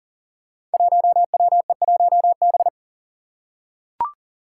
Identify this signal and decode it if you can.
Morse code, heard as a keyed tone, about 40 wpm, 700 Hz. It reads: 1WE1B
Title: Five Challenging Letter-Number Combinations 40wpm